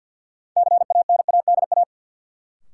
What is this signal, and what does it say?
Morse code, heard as a keyed tone, about 47 wpm, 700 Hz. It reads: CANADA
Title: Sets of 5 Words - New Word Drill - 50wpm